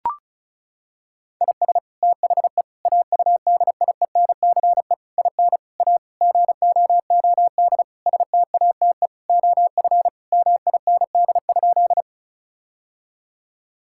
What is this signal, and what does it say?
Morse code, heard as a keyed tone, about 34 wpm, 700 Hz. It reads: IS THE AUDIENCE IN A GOOD STATE OF MIND?
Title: Sentences from Top 1000 Words - Review Part 5 - 35wpm